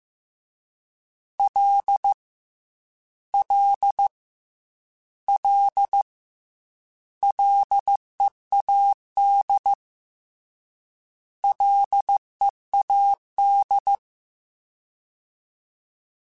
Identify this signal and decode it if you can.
Morse code, heard as a keyed tone, about 15 wpm, 800 Hz. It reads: L L L LEAD LEAD
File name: Combinations / Combinations-mp3s / Combinations@13